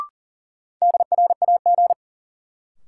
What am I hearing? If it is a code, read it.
Morse code, heard as a keyed tone, about 40 wpm, 700 Hz. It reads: DRAG
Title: Sentences from Top 1500 Words - New Word Drill Alt - 40wpm